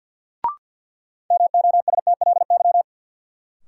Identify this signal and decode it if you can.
Morse code, heard as a keyed tone, about 53 wpm, 700 Hz. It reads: MOSTLY
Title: Sentences from Top 1300 Words - New Word Drill Alt - 50wpm